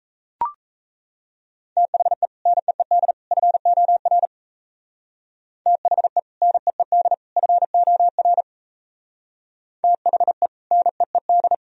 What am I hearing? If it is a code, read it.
Morse code, heard as a keyed tone, about 35 wpm, 700 Hz. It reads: THE NEED FOR THE NEED FOR THE NEED